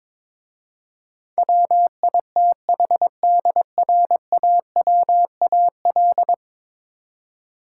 Morse code, heard as a keyed tone, about 22 wpm, 700 Hz. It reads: WITHDRAWAL
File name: practice / 22wpm / 10 Letter Words - First 3 Letters Given - 22wpm